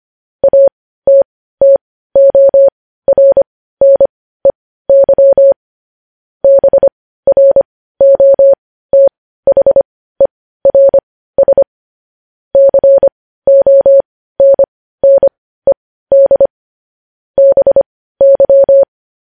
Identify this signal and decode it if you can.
Morse code, heard as a keyed tone, about 25 wpm, spaced as for 15 wpm, 550 Hz. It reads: ATTORNEY BROTHERS CONNED BY
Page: news-headlines-in-morse-code-at-15-wpm